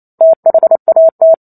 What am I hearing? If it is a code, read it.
Morse code, heard as a keyed tone, about 28 wpm, 650 Hz. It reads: THAT